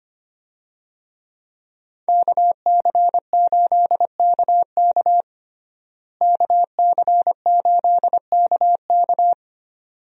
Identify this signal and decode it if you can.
Morse code, heard as a keyed tone, about 25 wpm, 700 Hz. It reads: KC8KK KC8KK
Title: USA Call Signs - 5 Characters - Repeated 2x - 25wpm